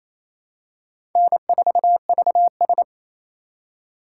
Morse code, heard as a keyed tone, about 28 wpm, 700 Hz. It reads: N4VS
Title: USA Call Signs - 4 Characters - 2x Character Spacing - 28wpm